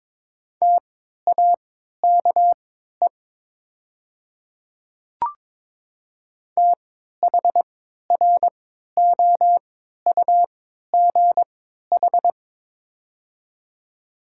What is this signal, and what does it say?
Morse code, heard as a keyed tone, about 22 wpm, spaced as for 13 wpm, 700 Hz. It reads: TAKE THROUGH
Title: 3x Character Spacing - Top 200 Words - 22wpm